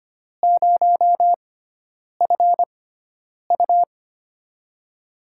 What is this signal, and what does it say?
Morse code, heard as a keyed tone, about 25 wpm, spaced as for 9 wpm, 700 Hz. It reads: 0FU
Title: USA Call Signs - 6x Character Spacing - 25wpm